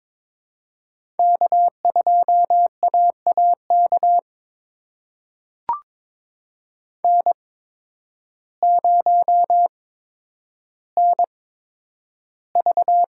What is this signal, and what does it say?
Morse code, heard as a keyed tone, about 22 wpm, spaced as for 10 wpm, 700 Hz. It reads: K2AAK N0NV
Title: USA Call Signs - 5 Characters - 8x Character Spacing - 22wpm